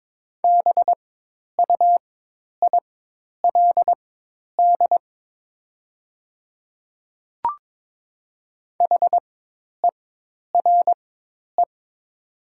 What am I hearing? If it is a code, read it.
Morse code, heard as a keyed tone, about 22 wpm, spaced as for 10 wpm, 700 Hz. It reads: BUILD HERE
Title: 4x Character Spacing - Top 200 Words - 22wpm